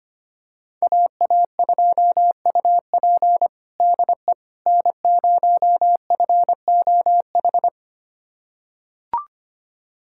Morse code, heard as a keyed tone, about 25 wpm, 700 Hz. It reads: AA2UP DE N0FOH
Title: USA Callsign DE Callsign - 5 Characters - 25wpm